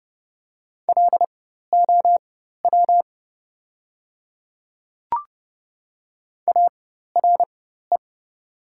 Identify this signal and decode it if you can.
Morse code, heard as a keyed tone, about 30 wpm, spaced as for 14 wpm, 690 Hz. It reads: LOW ARE
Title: 4x Character Spacing - Top 500 Words - 30wpm